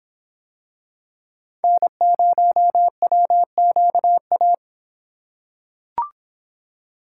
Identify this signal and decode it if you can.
Morse code, heard as a keyed tone, about 24 wpm, 700 Hz. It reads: N0WQA